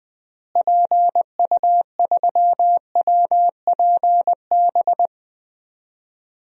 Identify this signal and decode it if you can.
Morse code, heard as a keyed tone, about 20 wpm, 700 Hz. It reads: PU3WPB
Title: DXCC-100 Call Signs - 6 Characters - Repeated 3x - 20wpm